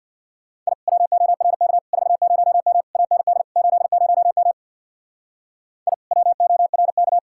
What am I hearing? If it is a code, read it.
Morse code, heard as a keyed tone, about 62 wpm, 700 Hz. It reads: I WORK 40M AND 80M I WORK
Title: Speed-Racing Common QSO Phrases - 50wpm